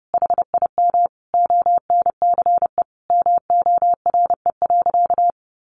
Morse code, heard as a keyed tone, about 30 wpm, 700 Hz. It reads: HIM ONCE MORE.
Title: Sentences from Top 1100 Words 30wpm